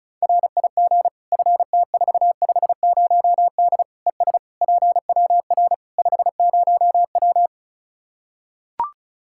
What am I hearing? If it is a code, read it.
Morse code, heard as a keyed tone, about 34 wpm, 700 Hz. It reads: RIG FT450D ES PWR 50W
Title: K1USN SST QSO Practice 35wpm